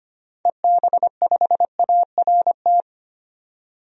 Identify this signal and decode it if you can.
Morse code, heard as a keyed tone, about 25 wpm, 700 Hz. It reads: EB5ART
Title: DXCC-100 Call Signs - 6 Characters - 25wpm